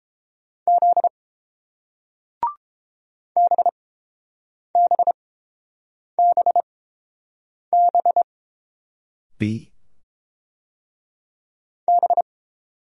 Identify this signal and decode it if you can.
Morse code, heard as a keyed tone, about 30 wpm, spaced as for 7 wpm, 700 Hz. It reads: Z BBBB B